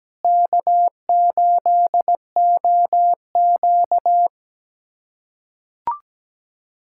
Morse code, heard as a keyed tone, about 17 wpm, 700 Hz. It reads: K8OQ
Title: USA Callsign DE Callsign - 4 Characters - 17wpm